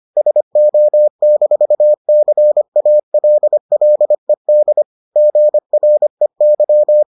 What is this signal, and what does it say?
Morse code, heard as a keyed tone, about 25 wpm, 600 Hz. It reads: SO-CALLED GREY